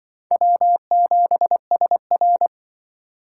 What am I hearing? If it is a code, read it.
Morse code, heard as a keyed tone, about 24 wpm, 690 Hz. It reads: W7SR
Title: Speed Racing - USA 1x2 Call Signs 20wpm